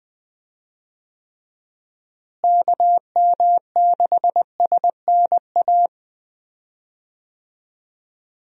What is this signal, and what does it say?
Morse code, heard as a keyed tone, about 20 wpm, 700 Hz. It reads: KM6SNA
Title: USA Call Signs - 6 Characters - Repeated 3x - 20wpm